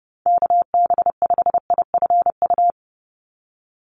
Morse code, heard as a keyed tone, about 30 wpm, 700 Hz. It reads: KB5IFU
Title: USA Call Signs 30wpm